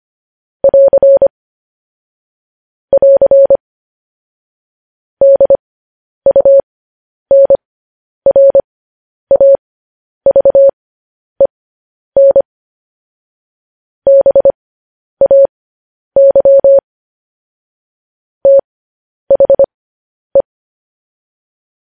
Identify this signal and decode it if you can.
Morse code, heard as a keyed tone, about 25 wpm, spaced as for 10 wpm, 550 Hz. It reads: <AR> <AR> DUNRAVEN BAY THE
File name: podcast / News headlines-1x-25x10